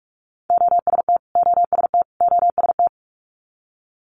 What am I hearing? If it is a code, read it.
Morse code, heard as a keyed tone, about 46 wpm, 700 Hz. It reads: OST OST OST
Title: Multi-letter Phonemes Repeated 3x 45wpm